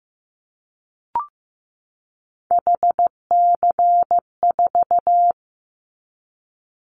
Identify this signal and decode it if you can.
Morse code, heard as a keyed tone, about 15 wpm, 700 Hz. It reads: HC4
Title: Callsign Prefixes for Top 100 DXCC - 15wpm